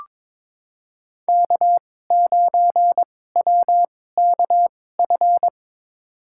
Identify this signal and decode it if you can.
Morse code, heard as a keyed tone, about 22 wpm, spaced as for 16 wpm, 700 Hz. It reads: K9WKF
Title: USA Call Signs - 5 Characters - 2x Character Spacing - 22wpm